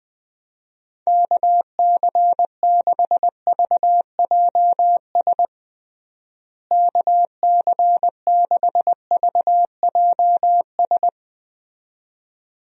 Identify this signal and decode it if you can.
Morse code, heard as a keyed tone, about 21 wpm, 700 Hz. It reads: KC6VJS KC6VJS
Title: USA Call Signs - 6 Characters - Repeated 2x - 20wpm